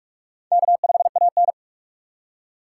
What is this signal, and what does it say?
Morse code, heard as a keyed tone, about 40 wpm, 700 Hz. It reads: K5AN